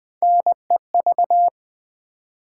Morse code, heard as a keyed tone, about 20 wpm, 700 Hz. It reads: NEV
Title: Build A Word - 1 Letter At A Time - 5-Letter Words - 20wpm